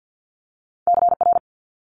Morse code, heard as a keyed tone, about 50 wpm, 700 Hz. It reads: CR
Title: Multi-letter Phonemes Quick Fire 50wpm